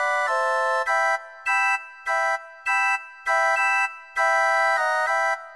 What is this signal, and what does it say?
There is no code.